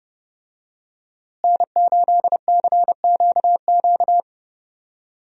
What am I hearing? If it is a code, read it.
Morse code, heard as a keyed tone, about 30 wpm, 700 Hz. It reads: N8CQQ
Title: USA Call Signs - 5 Characters - Repeated 3x - 30wpm